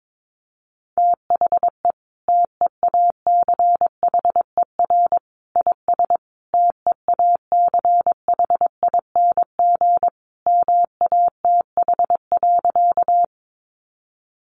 Morse code, heard as a keyed tone, about 22 wpm, 700 Hz. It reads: THE TEACHER IS TEACHING MATH.
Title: Easy Phrases 22wpm